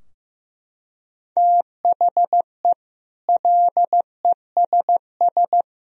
Morse code, heard as a keyed tone, about 16 wpm, 700 Hz. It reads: THE LESS
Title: Sentences from Top 1000 Words - Review Part 2 - 15wpm